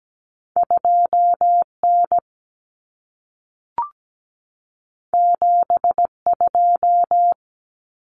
Morse code, heard as a keyed tone, about 17 wpm, 700 Hz. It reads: 2N 72
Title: Common Cut Numbers - 2-Digit 17wpm